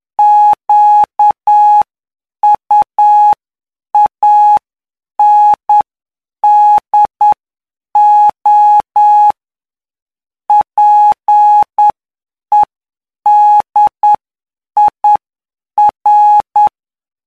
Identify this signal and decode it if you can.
Morse code, heard as a keyed tone, about 9 wpm, spaced as for 7 wpm, 840 Hz. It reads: QUANDO PEDIR
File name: cw / py2lcd / 19-PY2LCD-Li��o 28